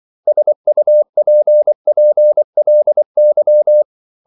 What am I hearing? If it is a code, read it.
Morse code, heard as a keyed tone, about 24 wpm, 600 Hz. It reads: SUPPLY
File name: CW / Practice / 500MostCommonEnglishWords24